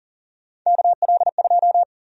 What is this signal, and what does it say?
Morse code, heard as a keyed tone, about 40 wpm, 700 Hz. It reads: KL2
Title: USA Call Sign Prefixes - 40wpm